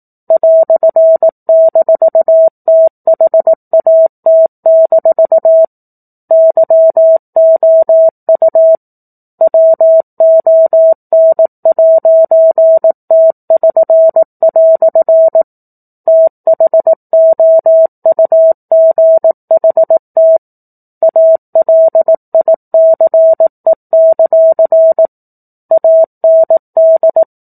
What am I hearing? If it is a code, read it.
Morse code, heard as a keyed tone, about 18 wpm, 650 Hz. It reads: "-THAT- YOU WON'T<SN>" THOUGHT ALICE; AND
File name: 04-Chapter-04 18 WPM